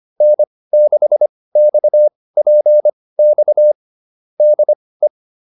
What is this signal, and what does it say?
Morse code, heard as a keyed tone, about 25 wpm, spaced as for 18 wpm, 600 Hz. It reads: N6XPX DE
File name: QSO306 18